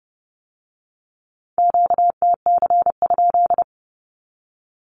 Morse code, heard as a keyed tone, about 30 wpm, 700 Hz. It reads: QTC?
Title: Q-Codes with Spoken Meaning - 30wpm